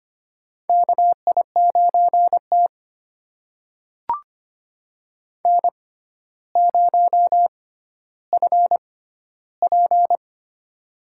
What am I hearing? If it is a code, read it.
Morse code, heard as a keyed tone, about 25 wpm, spaced as for 13 wpm, 700 Hz. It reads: KI9T N0FP
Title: USA Call Signs - 4 Characters - 6x Character Spacing - 25wpm